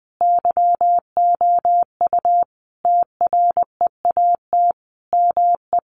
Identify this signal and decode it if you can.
Morse code, heard as a keyed tone, about 19 wpm, 700 Hz. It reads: YOU TREAT ME
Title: Sentences from Top 1300 Words 20wpm